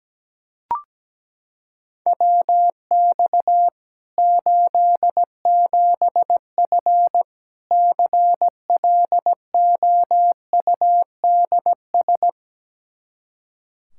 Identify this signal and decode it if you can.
Morse code, heard as a keyed tone, about 17 wpm, 700 Hz. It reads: WX 87F CLOUDS
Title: K1USN SST QSO Practice 17wpm